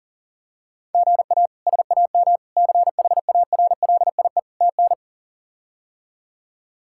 Morse code, heard as a keyed tone, about 40 wpm, 700 Hz. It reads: GA SAM CHARLIE TN
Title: K1USN SST QSO Practice - 40wpm